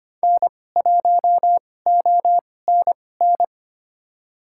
Morse code, heard as a keyed tone, about 25 wpm, spaced as for 18 wpm, 700 Hz. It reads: N1ONN
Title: USA Call Signs - 5 Characters - 2x Character Spacing - 25wpm